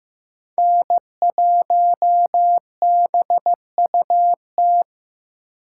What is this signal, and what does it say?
Morse code, heard as a keyed tone, about 15 wpm, 700 Hz. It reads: N1BUT
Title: USA Call Signs - 6x Character Spacing - 15wpm